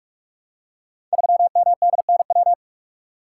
Morse code, heard as a keyed tone, about 44 wpm, 700 Hz. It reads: 3MDNW
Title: Code Groups - 5 Letter:Numbers - 45wpm